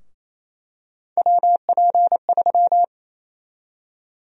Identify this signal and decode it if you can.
Morse code, heard as a keyed tone, about 28 wpm, 700 Hz. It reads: WP3
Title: USA Call Sign Prefixes - 28wpm